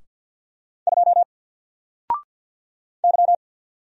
Morse code, heard as a keyed tone, about 48 wpm, 700 Hz. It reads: 2 Y
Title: Single Character (Letter:Number:Punctuation:Prosign) Alt 50wpm